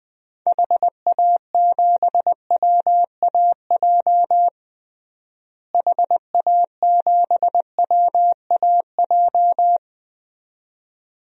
Morse code, heard as a keyed tone, about 20 wpm, 700 Hz. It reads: HA7WAJ HA7WAJ
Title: DXCC-100 Call Signs - 6 Characters - Repeated 2x - 20wpm